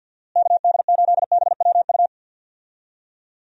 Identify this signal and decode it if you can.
Morse code, heard as a keyed tone, about 46 wpm, 700 Hz. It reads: KD8BWU
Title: USA Call Signs - 6 Characters - Repeated 2x - 50wpm